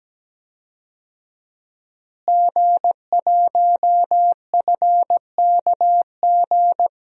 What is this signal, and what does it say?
Morse code, heard as a keyed tone, about 17 wpm, 700 Hz. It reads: G1FKG
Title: DXCC-100 Call Signs - 5 Characters - Repeated 2x - 17wpm